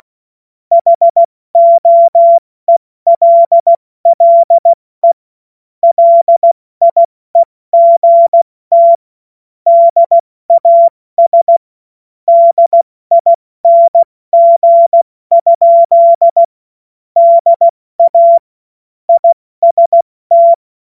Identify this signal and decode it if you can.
Morse code, heard as a keyed tone, about 16 wpm, 680 Hz. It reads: HOELLE LIEGT DAS DING? DA IST
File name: Morsekurs / Witze / Witz 02. August 2022 70 F80